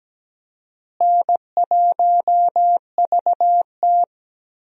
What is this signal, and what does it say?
Morse code, heard as a keyed tone, about 17 wpm, 700 Hz. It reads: N1VT
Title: USA Call Signs - 4 Characters - Repeated 2x - 17wpm